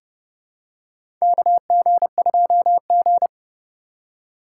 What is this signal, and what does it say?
Morse code, heard as a keyed tone, about 30 wpm, 700 Hz. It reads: KG2G
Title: USA Call Signs - 4x Character Spacing - 30wpm